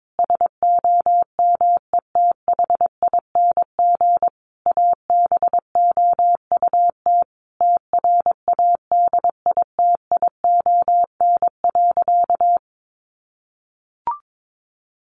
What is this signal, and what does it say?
Morse code, heard as a keyed tone, about 22 wpm, 700 Hz. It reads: SOMETHING ABOUT TRADITION.
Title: Sentences from Top 1500 Words 22wpm